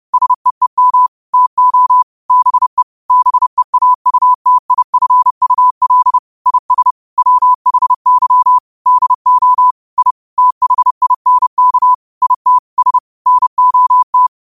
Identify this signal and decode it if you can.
Morse code, heard as a keyed tone, about 30 wpm, 1000 Hz. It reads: SEEM TO BE BEAUTIFUL IS WHY DO I THINK IT S NOT
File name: qotdfiles / 30wpm